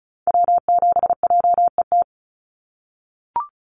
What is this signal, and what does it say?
Morse code, heard as a keyed tone, about 36 wpm, 700 Hz. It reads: W7JET
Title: NA SOTA Activator Callsigns 35wpm